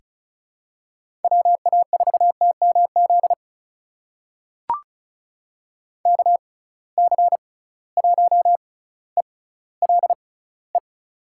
Morse code, heard as a keyed tone, about 36 wpm, spaced as for 18 wpm, 700 Hz. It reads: WA4TMZ KC1ELE
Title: USA Call Signs - 6 Characters - 6x Character Spacing - 35wpm